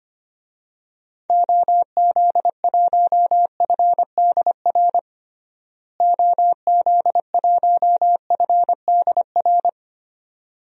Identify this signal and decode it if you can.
Morse code, heard as a keyed tone, about 25 wpm, 700 Hz. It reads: OZ1FDR OZ1FDR
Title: DXCC-100 Call Signs - 6 Characters - Repeated 2x - 25wpm